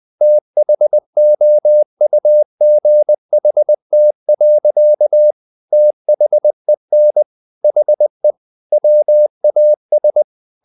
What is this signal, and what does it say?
Morse code, heard as a keyed tone, about 20 wpm, 600 Hz. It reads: THOUGHT. THEN HE WAS
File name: the old man and the sea part 2 20wpm